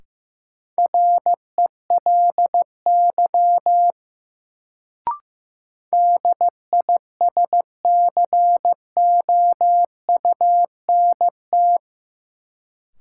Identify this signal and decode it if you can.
Morse code, heard as a keyed tone, about 15 wpm, 700 Hz. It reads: RELY DISCOUNT
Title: Sentences from Top 1200 Words - New Word Drill - 15wpm